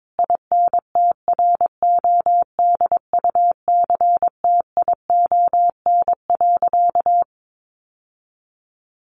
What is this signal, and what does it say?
Morse code, heard as a keyed tone, about 22 wpm, 700 Hz. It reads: INTRODUCTION.
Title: Sentences from Top 1000 Words 22wpm